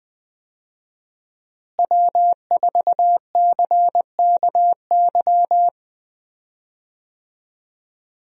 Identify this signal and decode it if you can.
Morse code, heard as a keyed tone, about 20 wpm, 700 Hz. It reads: W4CKY
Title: USA Call Signs - 5 Characters - Repeated 3x - 20wpm